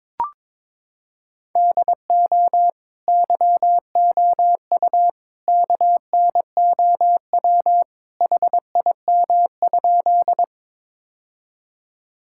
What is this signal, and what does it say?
Morse code, heard as a keyed tone, about 22 wpm, 700 Hz. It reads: DO YOU KNOW HIM?
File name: practice / 22wpm / Sentences with 2-Word Phrases - 22wpm